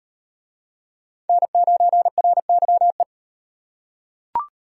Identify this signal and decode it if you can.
Morse code, heard as a keyed tone, about 38 wpm, 700 Hz. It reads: N9RYE